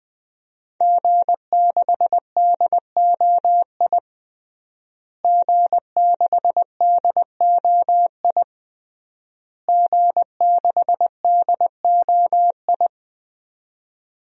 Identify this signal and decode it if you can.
Morse code, heard as a keyed tone, about 20 wpm, 700 Hz. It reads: G6DOI G6DOI G6DOI
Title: DXCC-100 Call Signs - 5 Characters - Repeated 3x - 20wpm